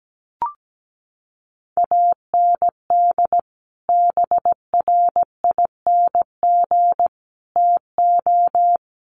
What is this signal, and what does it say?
Morse code, heard as a keyed tone, about 17 wpm, 700 Hz. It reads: AND BRING TO
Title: Sets of 5 Words Encore 17wpm